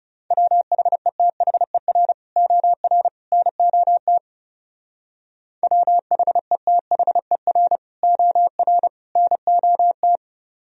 Morse code, heard as a keyed tone, about 35 wpm, 700 Hz. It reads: WHETHER OR NOT WHETHER OR NOT
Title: Speed-Racing - Sets of 3 Words 30wpm